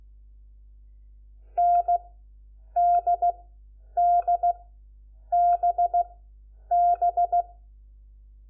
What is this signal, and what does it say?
Morse code, heard as a keyed tone, about 16 wpm, 700 Hz. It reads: N D D B B
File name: Lecciones / CLASE Nº4